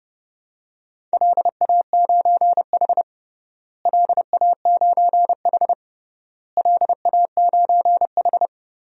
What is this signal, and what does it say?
Morse code, heard as a keyed tone, about 30 wpm, 700 Hz. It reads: LA9H LA9H LA9H
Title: DXCC-100 Call Signs - 4 Characters - Repeated 3x - 30wpm